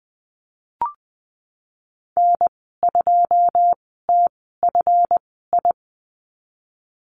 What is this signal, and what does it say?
Morse code, heard as a keyed tone, about 19 wpm, spaced as for 14 wpm, 700 Hz. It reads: N2TFI
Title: 167 Lesson 27g - Number 7 - Call Signs 20wpm